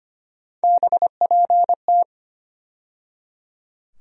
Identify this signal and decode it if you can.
Morse code, heard as a keyed tone, about 25 wpm, 700 Hz. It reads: BPT
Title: USA Call Sign Postfixes - 3 Letters 25wpm